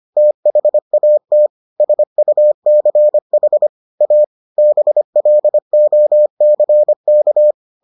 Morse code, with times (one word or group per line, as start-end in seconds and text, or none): 0.2-1.5 THAT
1.8-3.7 SUCH
4.0-4.2 A
4.6-7.5 BLOCK